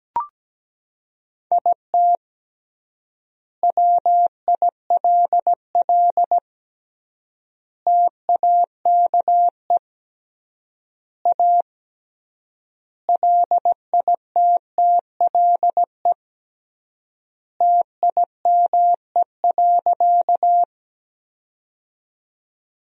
Morse code, heard as a keyed tone, about 17 wpm, 700 Hz. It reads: IT WILL TAKE A LITTLE TIME.
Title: 3x Word-Spacing - Sentences from Top 200 Words - 17wpm